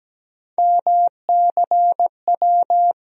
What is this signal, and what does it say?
Morse code, heard as a keyed tone, about 17 wpm, 700 Hz. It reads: MCW
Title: USA Call Sign Postfixes - 3 Letters 17wpm